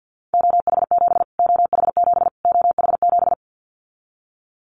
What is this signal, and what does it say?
Morse code, heard as a keyed tone, about 50 wpm, 700 Hz. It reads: OH7 OH7 OH7
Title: Callsign Prefixes for Top 25 DXCC - Repeated 3x - 50wpm